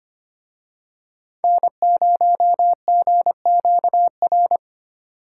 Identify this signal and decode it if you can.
Morse code, heard as a keyed tone, about 28 wpm, 700 Hz. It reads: N0GQR
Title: Speed Racing - USA 1x3 Call Signs 25wpm